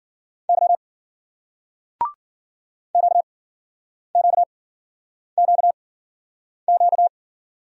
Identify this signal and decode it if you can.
Morse code, heard as a keyed tone, about 53 wpm, spaced as for 9 wpm, 700 Hz. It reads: Y QQQQ